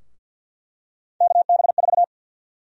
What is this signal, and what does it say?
Morse code, heard as a keyed tone, about 47 wpm, 700 Hz. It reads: KB4
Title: USA Call Sign Prefixes - 50wpm